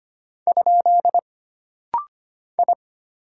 Morse code, heard as a keyed tone, about 25 wpm, 700 Hz. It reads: ? I
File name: Single Character (Letter:Number:Punctuation:Prosign) - Rapid-Fire - 25wpm